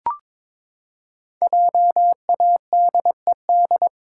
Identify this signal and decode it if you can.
Morse code, heard as a keyed tone, about 22 wpm, 700 Hz. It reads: JADED
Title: Pangram Challenge v2 - 22wpm